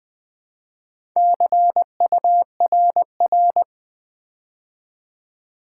Morse code, heard as a keyed tone, about 20 wpm, 700 Hz. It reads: CURR
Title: Build A Word - 2 Letters At A Time - 8-Letter Words - 20wpm